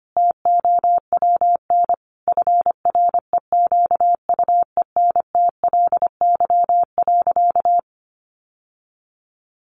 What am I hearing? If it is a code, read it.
Morse code, heard as a keyed tone, about 25 wpm, 700 Hz. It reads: TOWN FREQUENTLY.
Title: Sentences from Top 1300 Words 25wpm